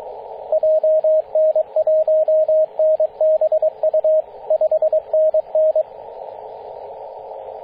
Morse code, heard as a keyed tone, about 22 wpm, 630 Hz, with radio noise. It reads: JN1NBU 5NN
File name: 9m0o-24-891mhz-cw